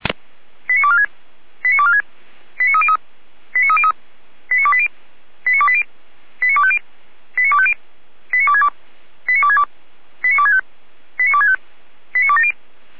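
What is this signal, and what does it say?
Five-tone selective call (ZVEI), heard as quick runs of five tones, tones of about 70 ms. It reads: 89246 89246 89303 89303 89280 89280 89250 89250 89262 89262 89265 89265 89270
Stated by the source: ZVEI five-tone selective call 89169, received with an SDR.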